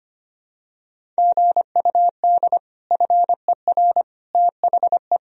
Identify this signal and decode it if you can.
Morse code, heard as a keyed tone, about 25 wpm, 700 Hz. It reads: GUD FER THE
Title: Sentences from Top 300 Words and Abbreviations 25wpm